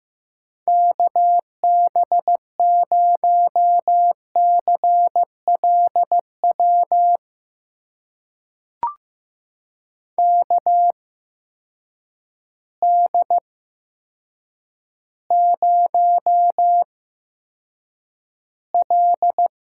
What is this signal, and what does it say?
Morse code, heard as a keyed tone, about 15 wpm, spaced as for 8 wpm, 700 Hz. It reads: KB0CLW KD0L
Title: USA Call Signs - 6 Characters - 8x Character Spacing - 15wpm